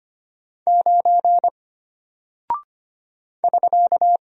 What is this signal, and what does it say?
Morse code, heard as a keyed tone, about 25 wpm, 700 Hz. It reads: 9 <SK>
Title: Single Character (Letter:Number:Punctuation:Prosign) Alt 25wpm